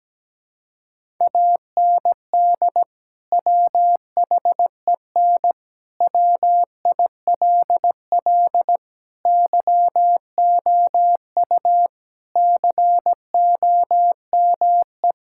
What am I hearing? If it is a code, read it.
Morse code, heard as a keyed tone, about 17 wpm, 700 Hz. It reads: AND WHEN WILL YOU COME
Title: Sentences from Top 1600 Words 17wpm